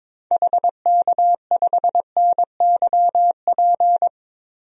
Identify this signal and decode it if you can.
Morse code, heard as a keyed tone, about 22 wpm, 700 Hz. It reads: HK5NYP
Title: DXCC-100 Call Signs - 6 Characters - Repeated 2x - 22wpm